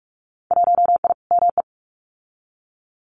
Morse code, heard as a keyed tone, about 46 wpm, 700 Hz. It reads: 1I ME
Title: Winter Field Day Exchanges 45wpm